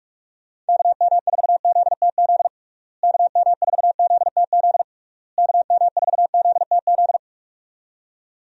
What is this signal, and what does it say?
Morse code, heard as a keyed tone, about 46 wpm, 700 Hz. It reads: KM4ZTZ KM4ZTZ KM4ZTZ
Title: USA Call Signs - 6 Characters - Repeated 3x - 45wpm